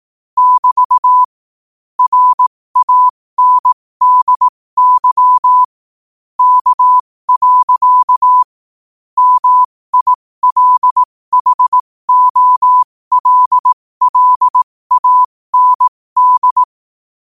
Morse code, heard as a keyed tone, about 18 wpm, spaced as for 15 wpm, 1000 Hz. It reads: <BT> RANDY K. MILHOLLAND
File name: qotdfiles / 15wpm